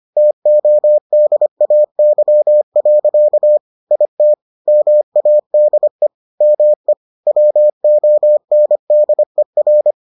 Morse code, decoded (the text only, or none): TODAY. IT MADE ME WONDER